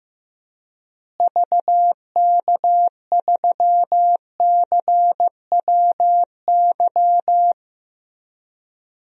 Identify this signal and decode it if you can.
Morse code, heard as a keyed tone, about 15 wpm, 700 Hz. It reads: VK3CWY
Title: DXCC-100 Call Signs - 6 Characters - Repeated 2x - 15wpm